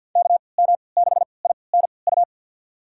Morse code, heard as a keyed tone, about 51 wpm, spaced as for 28 wpm, 700 Hz. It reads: KM6INU
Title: USA Call Signs - 3x Character Spacing - 50wpm